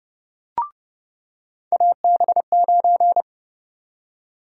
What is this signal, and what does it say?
Morse code, heard as a keyed tone, about 30 wpm, 700 Hz. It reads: AB9
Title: USA Call Sign Prefixes - 30wpm